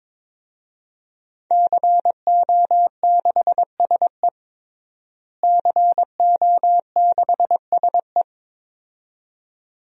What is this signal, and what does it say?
Morse code, heard as a keyed tone, about 21 wpm, 700 Hz. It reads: CO6SE CO6SE
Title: DXCC-100 Call Signs - 5 Characters - Repeated 2x - 22wpm